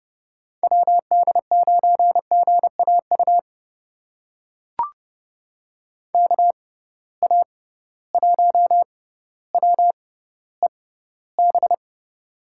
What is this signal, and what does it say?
Morse code, heard as a keyed tone, about 32 wpm, spaced as for 16 wpm, 700 Hz. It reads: WD9GAU KA1WEB
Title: USA Call Signs - 6 Characters - 6x Character Spacing - 30wpm